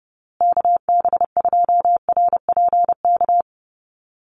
Morse code, heard as a keyed tone, about 30 wpm, 700 Hz. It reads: KB2RPK
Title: USA Call Signs Repeated 2x 30wpm